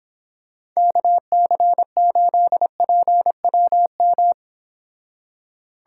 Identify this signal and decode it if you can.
Morse code, heard as a keyed tone, about 26 wpm, 700 Hz. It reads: KC8PWM